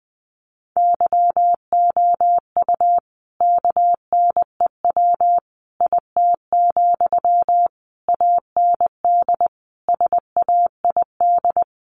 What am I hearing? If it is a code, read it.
Morse code, heard as a keyed tone, about 19 wpm, 700 Hz. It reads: YOU KNEW IT, AND SAID
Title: Sentences from Top 500 Words 20wpm